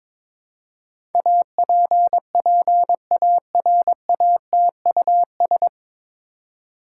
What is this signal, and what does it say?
Morse code, heard as a keyed tone, about 22 wpm, 700 Hz. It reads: APPARATUS
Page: Morse Code Ninja practice 22wpm